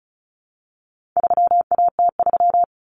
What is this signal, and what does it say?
Morse code, heard as a keyed tone, about 34 wpm, 700 Hz. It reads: 3AT3
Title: Common Cut Numbers - 4-Digit 35wpm